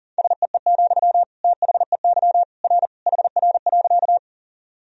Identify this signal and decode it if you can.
Morse code, heard as a keyed tone, about 40 wpm, 700 Hz. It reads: SEE, THEY R HR.
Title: Sentences from Top 200 Words and Abbreviations 40wpm